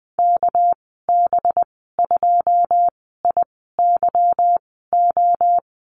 Morse code, heard as a keyed tone, about 20 wpm, spaced as for 14 wpm, 700 Hz. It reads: KB2IYO
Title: 231 Lesson 37g - Letter x - Call Signs 20wpm